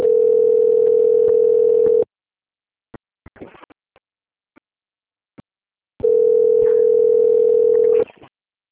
Ringback tone (North America), 440 + 480 Hz.
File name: rg-1600-unknown- - - .38881